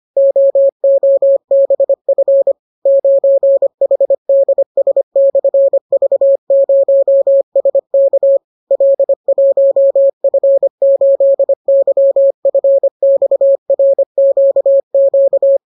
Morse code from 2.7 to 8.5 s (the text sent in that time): 9HDS/V0SK